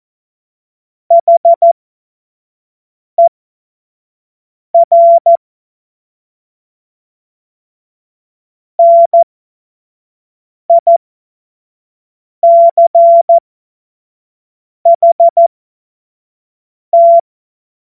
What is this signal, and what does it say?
Morse code, heard as a keyed tone, about 14 wpm, spaced as for 5 wpm, 680 Hz. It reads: HER NICHT
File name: Morsekurs / Witze / Witz 11. Januar 2022, 25 F70